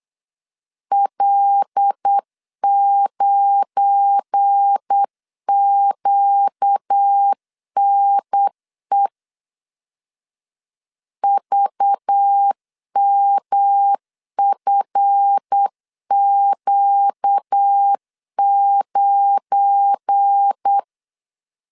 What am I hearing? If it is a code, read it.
Morse code, heard as a keyed tone, about 8 wpm, 800 Hz. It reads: L9QNE VMFQ9
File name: CW / DET / 1 9 1500C S C 8wpm 5653